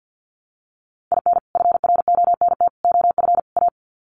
Morse code, heard as a keyed tone, about 50 wpm, 700 Hz. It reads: IN FRONT OF A
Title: Sets of 4 Words 50wpm